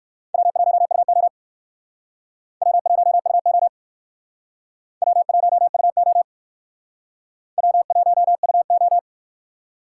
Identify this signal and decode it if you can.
Morse code, heard as a keyed tone, about 61 wpm, 700 Hz. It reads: W1UO W1UO W1UO W1UO